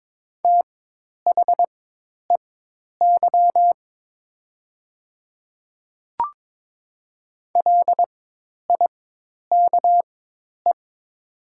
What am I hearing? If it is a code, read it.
Morse code, heard as a keyed tone, about 22 wpm, spaced as for 10 wpm, 700 Hz. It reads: THEY LIKE
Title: 4x Character Spacing - Top 200 Words - 22wpm